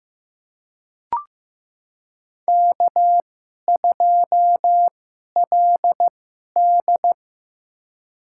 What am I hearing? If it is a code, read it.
Morse code, heard as a keyed tone, about 16 wpm, spaced as for 11 wpm, 700 Hz. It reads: K2LD
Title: USA Call Signs - 4 Characters - 2x Character Spacing - 15wpm